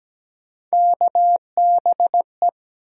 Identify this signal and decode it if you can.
Morse code, heard as a keyed tone, about 17 wpm, 700 Hz. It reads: KBE
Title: USA Call Sign Postfixes - 3 Letters 17wpm